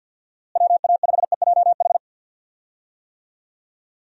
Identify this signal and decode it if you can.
Morse code, heard as a keyed tone, about 48 wpm, 700 Hz. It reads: WA5EJH
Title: USA Call Signs - 6 Characters - Repeated 2x - 50wpm